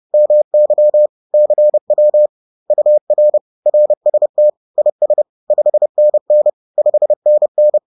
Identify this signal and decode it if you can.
Morse code, heard as a keyed tone, about 30 wpm, 600 Hz. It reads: MY CW UR RST IS 5NN 5NN